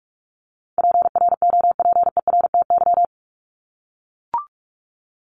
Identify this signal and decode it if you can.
Morse code, heard as a keyed tone, about 44 wpm, 700 Hz. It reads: PROPERTY
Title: Top 751-1000 Words 45wpm